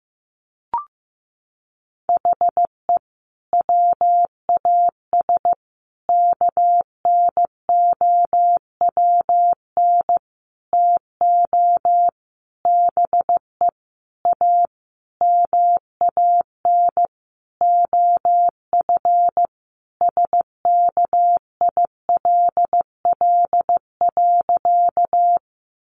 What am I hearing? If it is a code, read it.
Morse code, heard as a keyed tone, about 15 wpm, 700 Hz. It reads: HE WAS KNOWN TO BE A MAN OF SKILL.
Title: Sentences from Top 1200 Words 15wpm